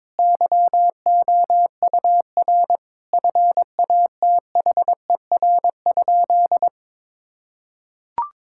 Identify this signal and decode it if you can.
Morse code, heard as a keyed tone, about 22 wpm, 700 Hz. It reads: YOUR FATHER?
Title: Sentences from Top 1000 Words - Review Part 5 - 22wpm